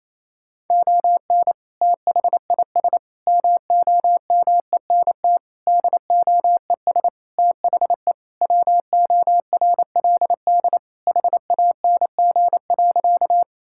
Morse code, heard as a keyed tone, about 28 wpm, 700 Hz. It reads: ON THIS MOMENT DOES THE WORLD HANG.
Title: Sentences from Top 1600 Words 28wpm